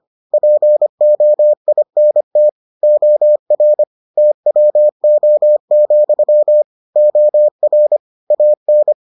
Morse code, heard as a keyed tone, about 25 wpm, 600 Hz. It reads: POINT OR TWO, OR AN